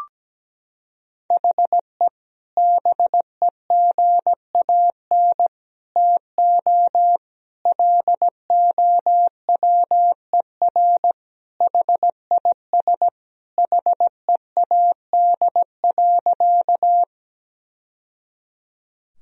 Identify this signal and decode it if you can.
Morse code, heard as a keyed tone, about 17 wpm, 700 Hz. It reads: HE BEGAN TO LOWER HIS HEAD.
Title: Sentences from Top 1000 Words - Review Part 4 - 17wpm